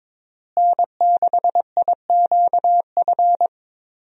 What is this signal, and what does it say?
Morse code, heard as a keyed tone, about 22 wpm, 700 Hz. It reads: N6IQF
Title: USA Call Signs - 5 Characters - Repeated 3x - 22wpm